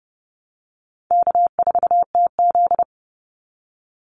Morse code, heard as a keyed tone, about 30 wpm, 700 Hz. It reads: K4TZ
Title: Easy Call Signs Repeated 3x - 30wpm